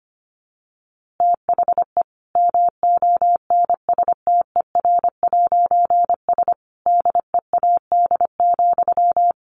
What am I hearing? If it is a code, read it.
Morse code, heard as a keyed tone, about 25 wpm, 700 Hz. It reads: THE MONSTER'S DEAD,
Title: A Sound Of Thunder 25wpm